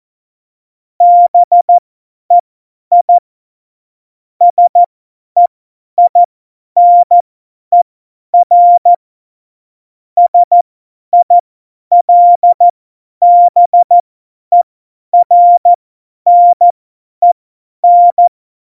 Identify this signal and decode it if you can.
Morse code, heard as a keyed tone, about 14 wpm, spaced as for 10 wpm, 700 Hz. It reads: BEI SEINER SILBERNEN